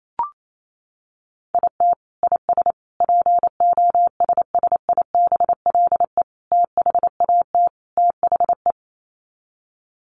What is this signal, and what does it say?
Morse code, heard as a keyed tone, about 28 wpm, 700 Hz. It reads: IT IS POSSIBLE THAT THE
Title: Sets of 5 Words Encore2 28wpm